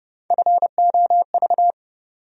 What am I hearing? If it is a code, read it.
Morse code, heard as a keyed tone, about 32 wpm, 700 Hz. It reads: FOV
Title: USA Call Sign Postfixes - 3 Letters 30wpm